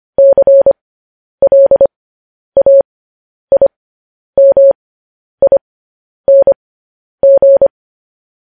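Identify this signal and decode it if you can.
Morse code, heard as a keyed tone, about 25 wpm, spaced as for 10 wpm, 550 Hz. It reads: CLAIMING
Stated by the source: Morse code news headlines practice, at 10 WPM.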